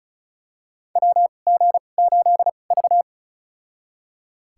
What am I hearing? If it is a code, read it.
Morse code, heard as a keyed tone, about 36 wpm, spaced as for 25 wpm, 700 Hz. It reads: WG8V
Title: USA Call Signs - 2x Character Spacing - 35wpm